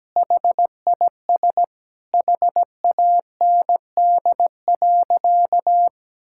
Morse code, heard as a keyed tone, about 17 wpm, 700 Hz. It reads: HIS HAND.
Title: Sentences from Top 1600 Words 17wpm